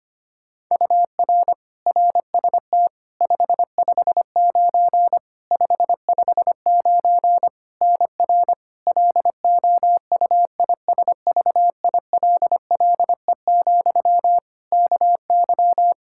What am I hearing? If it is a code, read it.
Morse code, heard as a keyed tone, about 25 wpm, 700 Hz. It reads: UR RST 559 559 NR LOUISVILLE, KY
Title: Practice QSO - Exchange 1 - Beginning - 25wpm